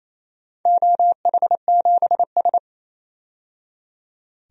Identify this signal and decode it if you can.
Morse code, heard as a keyed tone, about 28 wpm, 700 Hz. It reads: OH7S
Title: DXCC-100 Call Signs - 4 Characters - 28wpm